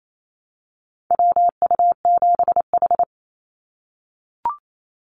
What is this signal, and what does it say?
Morse code, heard as a keyed tone, about 28 wpm, 700 Hz. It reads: WU7H
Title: NA SOTA Activator Callsigns 28wpm